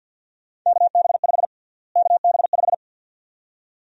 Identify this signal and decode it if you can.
Morse code, heard as a keyed tone, about 48 wpm, 700 Hz. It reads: KB5 KB5
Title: USA Call Sign Prefixes - Repeated 2x - 50wpm